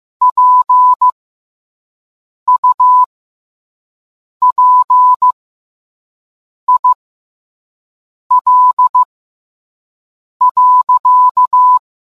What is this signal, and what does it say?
Morse code, heard as a keyed tone, about 15 wpm, spaced as for 5 wpm, 1000 Hz. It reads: PUPIL.